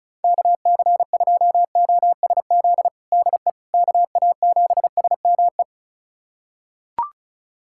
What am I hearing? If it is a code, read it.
Morse code, heard as a keyed tone, about 34 wpm, 700 Hz. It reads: KC2OSZ DE KA7SME
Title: USA Callsign DE Callsign - 6 Characters - 35wpm